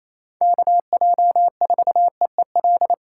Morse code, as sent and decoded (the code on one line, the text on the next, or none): -.- .--- ....- . .-..
KJ4EEL